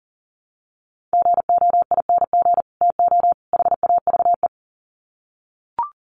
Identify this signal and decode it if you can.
Morse code, heard as a keyed tone, about 40 wpm, 700 Hz. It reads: GOING TO HAVE
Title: Sets of 3 Words 40wpm